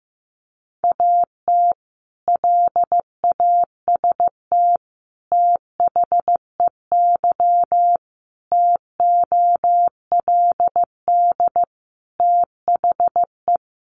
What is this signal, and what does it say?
Morse code, heard as a keyed tone, about 15 wpm, 700 Hz. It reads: AT LAST THEY TOLD THE
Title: Sentences from Top 1200 Words 15wpm